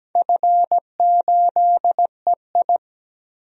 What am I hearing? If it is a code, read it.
Morse code, heard as a keyed tone, about 17 wpm, 700 Hz. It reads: F8EI
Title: DXCC-100 Call Signs - 4 Characters - 17wpm